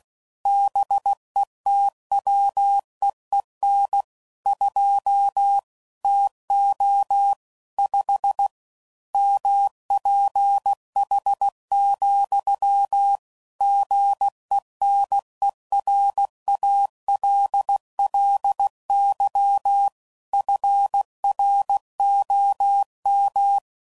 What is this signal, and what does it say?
Morse code, heard as a keyed tone, about 16 wpm, 800 Hz. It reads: BETWEEN 2 TO 5 MPH, GENERALLY FROM
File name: CW Code Groups @15WPM / Weather Sets / Weather 5 12